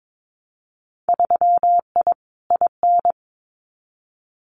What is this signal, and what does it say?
Morse code, heard as a keyed tone, about 22 wpm, 700 Hz. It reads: 3I IN
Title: Winter Field Day Exchanges 22wpm